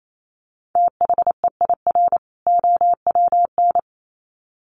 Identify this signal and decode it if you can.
Morse code, heard as a keyed tone, about 28 wpm, 700 Hz. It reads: THEIR OWN
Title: Sets of 2 Words 28wpm